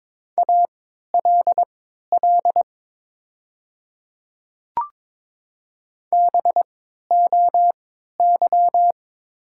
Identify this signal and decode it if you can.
Morse code, heard as a keyed tone, about 22 wpm, spaced as for 13 wpm, 700 Hz. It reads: ALL BOY
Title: 3x Character Spacing - Top 300 Words - 22wpm